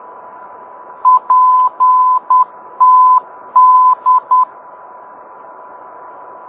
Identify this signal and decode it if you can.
Morse code, heard as a keyed tone, about 10 wpm, 1010 Hz. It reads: PTD